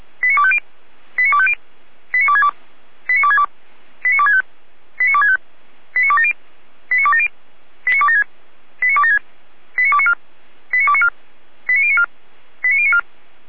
Five-tone selective call (ZVEI), heard as quick runs of five tones, tones of about 70 ms. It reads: 89250 89250 89262 89262 89265 89265 89270 89270 89276 89276 89294 89294 89094 89094
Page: SDR/ZVEI